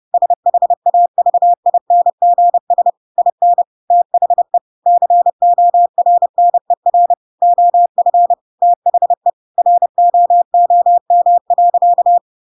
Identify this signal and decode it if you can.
Morse code, heard as a keyed tone, about 30 wpm, 700 Hz. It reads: SHAVINGS IN THE CORNER OF THE ROOM.
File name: Pinocchio / Pinocchio ch7 30 30 ik1hge